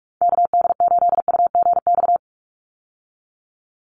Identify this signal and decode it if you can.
Morse code, heard as a keyed tone, about 44 wpm, 700 Hz. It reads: KD8UGX
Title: USA Call Signs 45wpm